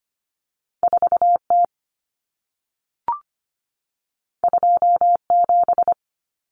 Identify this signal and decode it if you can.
Morse code, heard as a keyed tone, about 25 wpm, 700 Hz. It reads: 4T 27
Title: Common Cut Numbers - 2-Digit 25wpm